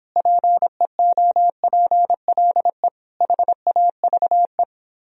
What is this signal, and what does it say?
Morse code, heard as a keyed tone, about 31 wpm, 700 Hz. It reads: PEOPLE HAVE